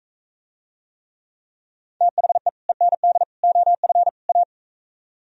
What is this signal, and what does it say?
Morse code, heard as a keyed tone, about 45 wpm, 700 Hz. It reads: THE END OF A